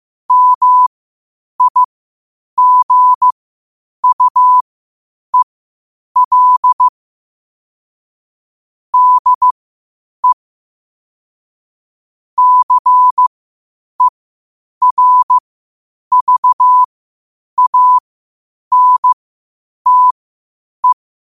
Morse code, heard as a keyed tone, about 15 wpm, spaced as for 8 wpm, 1000 Hz. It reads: MIGUEL DE CERVANTE
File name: qotdfiles / 8wpm